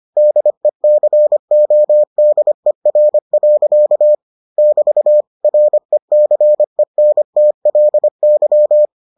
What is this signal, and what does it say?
Morse code, heard as a keyed tone, about 25 wpm, 600 Hz. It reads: DECODER. <BT> RECENTLY